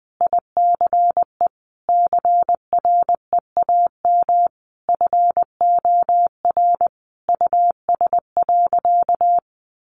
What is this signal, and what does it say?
Morse code, heard as a keyed tone, about 20 wpm, 700 Hz. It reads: ICE CREAM FOR US.
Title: Sentences from Top 1500 Words 20wpm